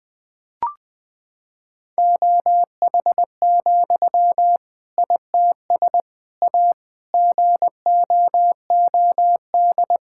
Morse code, heard as a keyed tone, about 21 wpm, 700 Hz. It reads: OH, ITS A GOOD
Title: Sentences from Top 1000 Words - Review Part 5 - 20wpm